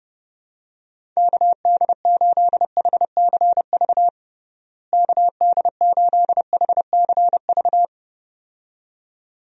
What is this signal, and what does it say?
Morse code, heard as a keyed tone, about 30 wpm, 700 Hz. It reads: KD8HCV KD8HCV
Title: USA Call Signs - 6 Characters - Repeated 2x - 30wpm